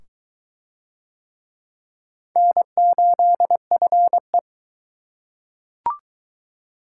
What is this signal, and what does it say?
Morse code, heard as a keyed tone, about 22 wpm, 700 Hz. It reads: N8FE